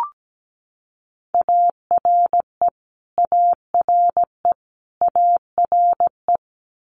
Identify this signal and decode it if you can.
Morse code, heard as a keyed tone, about 17 wpm, 700 Hz. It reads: ARE ARE ARE
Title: Multi-letter Phonemes Repeated 3x 17wpm